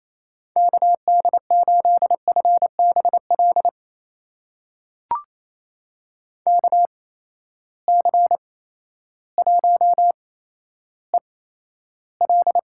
Morse code, heard as a keyed tone, about 28 wpm, spaced as for 13 wpm, 700 Hz. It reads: KD8FBL KC1EL
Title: USA Call Signs - 6 Characters - 8x Character Spacing - 28wpm